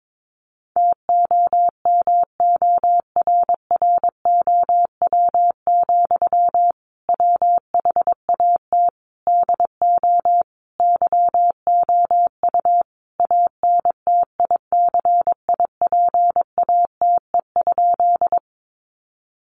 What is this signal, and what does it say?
Morse code, heard as a keyed tone, about 22 wpm, 700 Hz. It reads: TOMORROW, WHAT DO YOU ANTICIPATE?
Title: Sentences from Top 1500 Words 22wpm